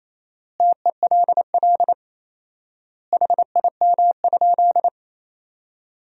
Morse code, heard as a keyed tone, about 28 wpm, 700 Hz. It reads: TELL HIM?
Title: 4x Word-Spacing - Sentences from Top 200 Words - 28wpm